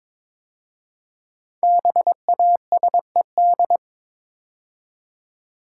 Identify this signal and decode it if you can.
Morse code, heard as a keyed tone, about 22 wpm, 700 Hz. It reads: BASED
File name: 5 Letter Words - First 3 Letters Given - 22wpm